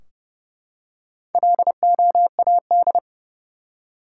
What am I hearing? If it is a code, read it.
Morse code, heard as a keyed tone, about 32 wpm, 700 Hz. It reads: LOAD
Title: Four Letter Words - 30wpm